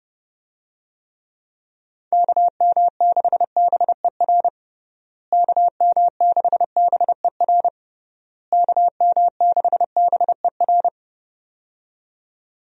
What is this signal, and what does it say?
Morse code, heard as a keyed tone, about 30 wpm, 700 Hz. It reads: KM6BER KM6BER KM6BER
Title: USA Call Signs - 6 Characters - Repeated 3x - 30wpm